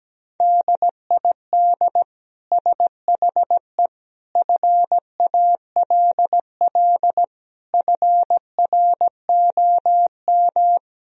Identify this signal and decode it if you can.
Morse code, heard as a keyed tone, about 17 wpm, 700 Hz. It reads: DID SHE FALL FROM
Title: Sentences from Top 1000 Words - Review Part 2 - 17wpm